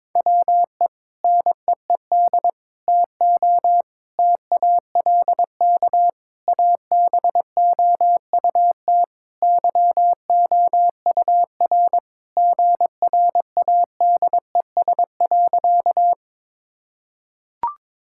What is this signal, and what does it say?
Morse code, heard as a keyed tone, about 22 wpm, 700 Hz. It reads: WE NEED TO TALK ABOUT YOUR GRADES.
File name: practice / 22wpm / Sentences with 3-Word Phrases - 22wpm